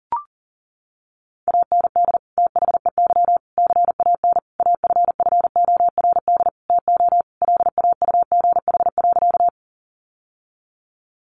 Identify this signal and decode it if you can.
Morse code, heard as a keyed tone, about 40 wpm, 700 Hz. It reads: AND THEY CAN AFFORD TO LAUGH.
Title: Sentences from Top 1100 Words 40wpm